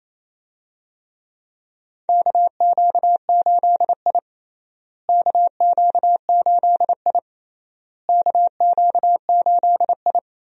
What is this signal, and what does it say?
Morse code, heard as a keyed tone, about 28 wpm, 700 Hz. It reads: KQ8I KQ8I KQ8I
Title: USA Call Signs - 4 Characters - Repeated 3x - 28wpm